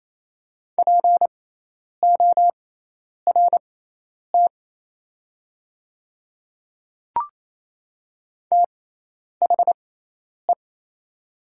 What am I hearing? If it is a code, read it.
Morse code, heard as a keyed tone, about 28 wpm, spaced as for 10 wpm, 690 Hz. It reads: PORT THE
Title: 6x Character Spacing - Top 200 Words - 28wpm